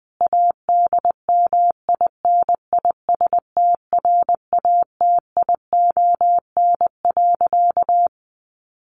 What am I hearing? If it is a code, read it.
Morse code, heard as a keyed tone, about 20 wpm, 700 Hz. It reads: ADMINISTRATION.
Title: Sentences from Top 1300 Words 20wpm